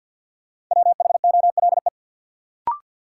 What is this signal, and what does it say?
Morse code, heard as a keyed tone, about 53 wpm, 700 Hz. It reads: WHOLE
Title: Sets of 2 Words - New Word Drill - 50wpm